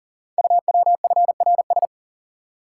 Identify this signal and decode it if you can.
Morse code, heard as a keyed tone, about 39 wpm, 700 Hz. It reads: UWFRS
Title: Code Groups - 5 Letters - 40wpm